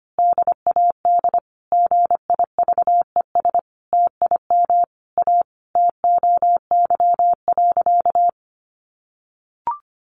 Morse code, heard as a keyed tone, about 25 wpm, 700 Hz. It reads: DAD GIVES TIM A TOY.
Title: Easy Phrases 25wpm